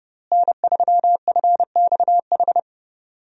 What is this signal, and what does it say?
Morse code, heard as a keyed tone, about 30 wpm, 700 Hz. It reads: N3FXH
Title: USA Call Signs - 5 Characters - Repeated 2x - 30wpm